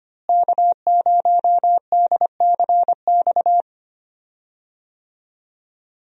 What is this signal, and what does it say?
Morse code, heard as a keyed tone, about 25 wpm, 700 Hz. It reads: K0DCX
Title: USA Call Signs - 5 Characters - Repeated 2x - 25wpm